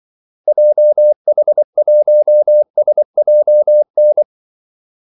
Morse code, heard as a keyed tone, about 24 wpm, 600 Hz. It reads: JH1SJN
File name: Callsigns-24wpm